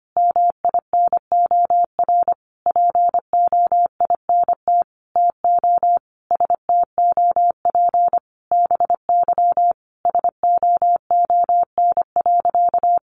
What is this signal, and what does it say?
Morse code, heard as a keyed tone, about 25 wpm, 700 Hz. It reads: MINOR POINT TO STOP BY SOON.
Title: Sentences from Top 1500 Words 25wpm